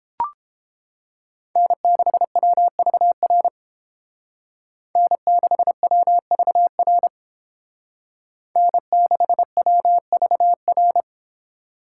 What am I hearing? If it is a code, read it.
Morse code, heard as a keyed tone, about 29 wpm, 700 Hz. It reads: N6WVR N6WVR N6WVR